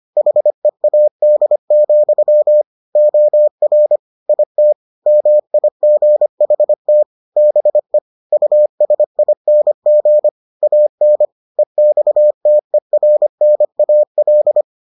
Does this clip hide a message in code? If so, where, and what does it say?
Yes: Morse code from 0.2 to 14.6 s, heard as a keyed tone, about 25 wpm, 600 Hz. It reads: HEAD, OR IT MIGHT BE USING AN EXTERNAL